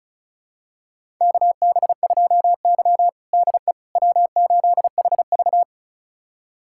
Morse code, heard as a keyed tone, about 36 wpm, 700 Hz. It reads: KB2Y DE W8HV
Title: USA Callsign DE Callsign - 4 Characters - 35wpm